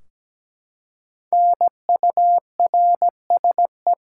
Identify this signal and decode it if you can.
Morse code, heard as a keyed tone, about 17 wpm, 700 Hz. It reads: NURSE
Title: Five Letter Words - 17wpm